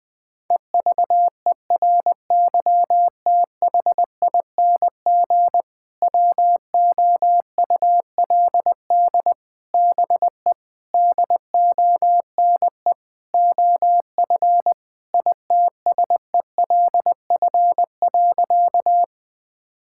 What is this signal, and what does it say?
Morse code, heard as a keyed tone, about 21 wpm, 700 Hz. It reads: EVERYTHING WOULD BE DONE OF ITSELF.
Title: Sentences from Top 1000 Words - Review Part 5 - 20wpm